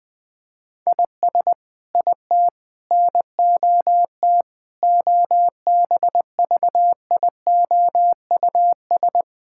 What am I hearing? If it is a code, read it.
Morse code, heard as a keyed tone, about 21 wpm, 700 Hz. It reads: IS IT NOT OBVIOUS
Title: Sentences from Top 1600 Words 20wpm